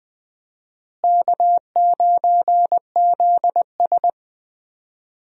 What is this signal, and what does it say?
Morse code, heard as a keyed tone, about 21 wpm, 700 Hz. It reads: K9ZS
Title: USA Call Signs - 4 Characters - Repeated 3x - 20wpm